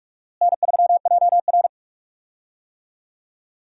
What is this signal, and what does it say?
Morse code, heard as a keyed tone, about 45 wpm, 700 Hz. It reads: N3JR
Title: USA Call Signs - 4 Characters - Repeated 3x - 45wpm